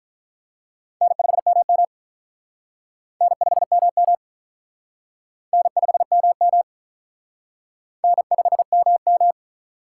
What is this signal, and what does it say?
Morse code, heard as a keyed tone, about 45 wpm, 700 Hz. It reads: N5MM N5MM N5MM N5MM